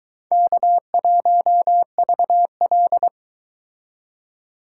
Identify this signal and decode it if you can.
Morse code, heard as a keyed tone, about 22 wpm, 700 Hz. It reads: K1VL